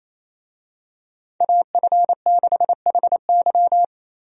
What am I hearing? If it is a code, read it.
Morse code, heard as a keyed tone, about 28 wpm, 700 Hz. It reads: AF6HY
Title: USA Call Signs - 5 Characters - 28wpm